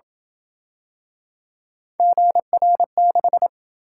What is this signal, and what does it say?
Morse code, heard as a keyed tone, about 30 wpm, 700 Hz. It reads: GR6